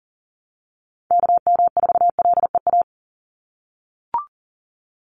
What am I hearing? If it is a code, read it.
Morse code, heard as a keyed tone, about 40 wpm, 690 Hz. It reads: KM4LEA
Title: USA Call Signs 40wpm